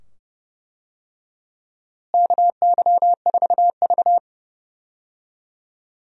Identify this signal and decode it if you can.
Morse code, heard as a keyed tone, about 32 wpm, 700 Hz. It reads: KY4V
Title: USA Call Signs - 4 Characters - Repeated 2x - 30wpm